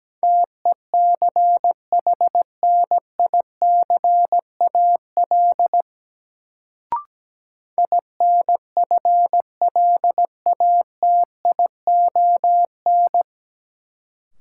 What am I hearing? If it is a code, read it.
Morse code, heard as a keyed tone, about 17 wpm, 700 Hz. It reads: TECHNICAL INFLATION
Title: Sentences from Top 1100 Words - New Word Drill Alt - 17wpm